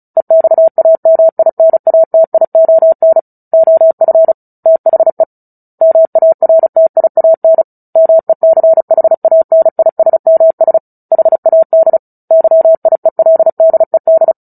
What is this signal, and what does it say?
Morse code, heard as a keyed tone, about 35 wpm, 650 Hz. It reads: EXAMINATION OF THE MARTIAN MECHANISMS HAD YIELDED
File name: Book Two / 09-Chapter 9 - 35 WPM